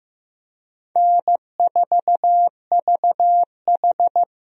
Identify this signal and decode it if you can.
Morse code, heard as a keyed tone, about 15 wpm, 700 Hz. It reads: N4VH
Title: USA Call Signs - 4 Characters - Repeated 3x - 15wpm